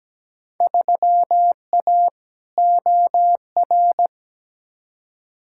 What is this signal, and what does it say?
Morse code, heard as a keyed tone, about 17 wpm, 700 Hz. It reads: 3A OR
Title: Field Day Exchanges - 17wpm